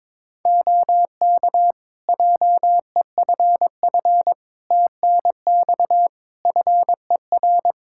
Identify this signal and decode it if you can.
Morse code, heard as a keyed tone, about 22 wpm, 700 Hz. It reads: OK JEFF TNX FER
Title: K1USN SST QSO Practice 22wpm